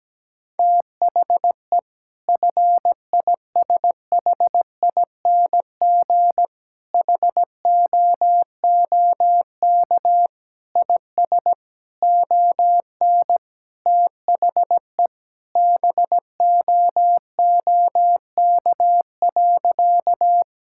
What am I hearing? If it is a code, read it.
Morse code, heard as a keyed tone, about 17 wpm, 700 Hz. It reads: THE FISHING HOOK IS ON THE BOOK.
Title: Sentences from Top 1600 Words 17wpm